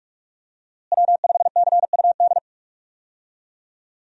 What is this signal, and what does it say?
Morse code, heard as a keyed tone, about 46 wpm, 700 Hz. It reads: W5CUD
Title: USA Call Signs - 5 Characters - Repeated 2x - 45wpm